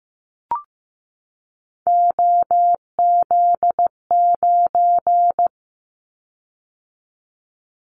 Morse code, heard as a keyed tone, about 15 wpm, 700 Hz. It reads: OZ9
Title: Callsign Prefixes for Top 100 DXCC - 15wpm